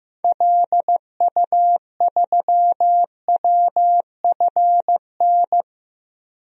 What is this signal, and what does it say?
Morse code, heard as a keyed tone, about 15 wpm, 700 Hz. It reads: LU3WFN
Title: DXCC-100 Call Signs - 6 Characters - 15wpm